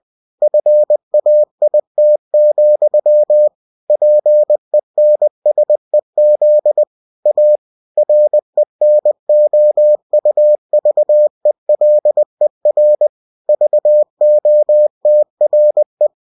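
Morse code, heard as a keyed tone, about 20 wpm, 600 Hz. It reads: FAIT, PENSEZ A RENOUVELER VOTRE
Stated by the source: Bulletin sent in CW.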